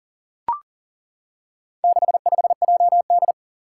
Morse code, heard as a keyed tone, about 40 wpm, 690 Hz. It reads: 65JD
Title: Four Challenging Letter-Number Combinations 40wpm